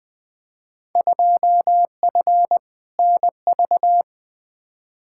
Morse code, heard as a keyed tone, about 20 wpm, 700 Hz. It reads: 2F NV
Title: Field Day Exchanges - 20wpm